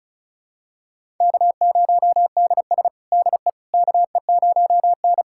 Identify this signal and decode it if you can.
Morse code, heard as a keyed tone, about 36 wpm, 700 Hz. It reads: K0DS DE KE0N
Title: USA Callsign DE Callsign - 4 Characters - 35wpm